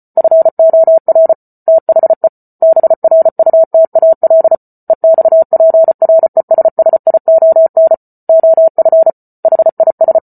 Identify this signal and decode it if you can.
Morse code, heard as a keyed tone, about 34 wpm, 650 Hz. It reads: FOR THE BRUTAL EXPRESSION OF HIS